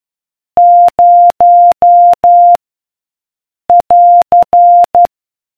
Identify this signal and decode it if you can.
Morse code, heard as a keyed tone, about 12 wpm, 700 Hz. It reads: 0 <AR>